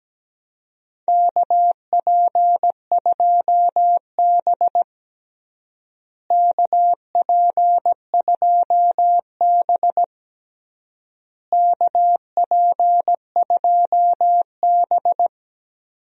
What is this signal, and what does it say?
Morse code, heard as a keyed tone, about 17 wpm, 700 Hz. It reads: KP2B KP2B KP2B
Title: DXCC-100 Call Signs - 4 Characters - Repeated 3x - 17wpm